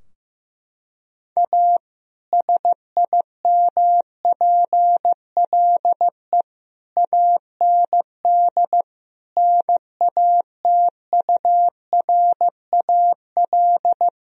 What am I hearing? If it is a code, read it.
Morse code, heard as a keyed tone, about 15 wpm, 700 Hz. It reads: A SIMPLE AND NATURAL
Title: Sentences from Top 1000 Words - Review Part 4 - 15wpm